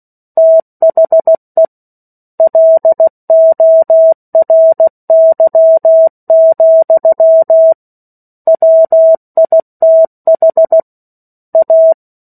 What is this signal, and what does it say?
Morse code, heard as a keyed tone, about 16 wpm, 650 Hz. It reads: THE LORY, WITH A
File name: 03-Chapter-03 16 WPM